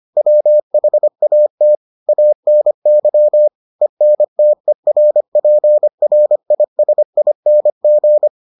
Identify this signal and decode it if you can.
Morse code, heard as a keyed tone, about 25 wpm, 600 Hz. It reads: WHAT ANY ENTERPRISING